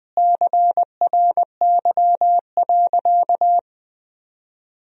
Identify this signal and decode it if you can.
Morse code, heard as a keyed tone, about 21 wpm, 700 Hz. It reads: CRY.
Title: Sentences from Top 500 Words and Abbreviations 20wpm